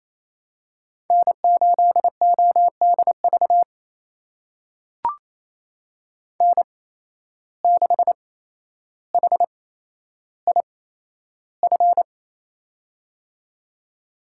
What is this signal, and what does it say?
Morse code, heard as a keyed tone, about 28 wpm, spaced as for 12 wpm, 700 Hz. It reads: N8ODV N6HIF
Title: USA Call Signs - 5 Characters - 8x Character Spacing - 28wpm